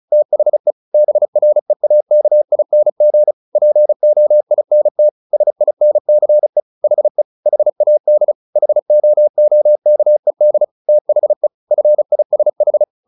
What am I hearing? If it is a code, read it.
Morse code, heard as a keyed tone, about 34 wpm, 600 Hz. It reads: THE BREAKING POINT SINCE HE HAD HOOKED THE FISH
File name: the old man and the sea / the old man and the sea 1 35wpm